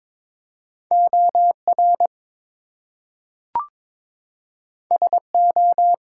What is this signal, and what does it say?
Morse code, heard as a keyed tone, about 22 wpm, 700 Hz. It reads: OR SO
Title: Two Letter Words - 22wpm